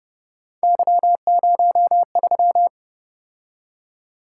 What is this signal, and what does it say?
Morse code, heard as a keyed tone, about 30 wpm, 700 Hz. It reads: Y03
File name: Speed-Racing - Three Letter-Number Combinations - 25wpm